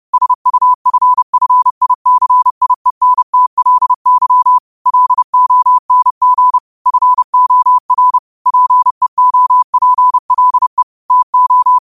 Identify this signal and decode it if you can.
Morse code, heard as a keyed tone, about 30 wpm, 1000 Hz. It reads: SUFFICIENTLY LONG FOR PEOPLE TO
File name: qotdfiles / 30wpm